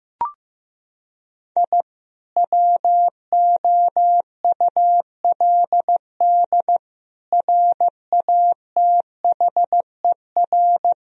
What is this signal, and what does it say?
Morse code, heard as a keyed tone, about 15 wpm, 700 Hz. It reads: I WOULD RATHER
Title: Sentences from Top 1000 Words - Review Part 4 - 15wpm